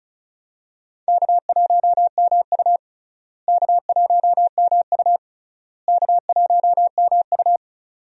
Morse code, heard as a keyed tone, about 36 wpm, 700 Hz. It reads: K1MU K1MU K1MU
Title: USA Call Signs - 4 Characters - Repeated 3x - 35wpm